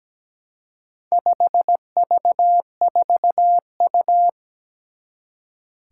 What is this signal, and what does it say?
Morse code, heard as a keyed tone, about 17 wpm, 700 Hz. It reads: 5V4U
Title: Four Challenging Letter-Number Combinations 17wpm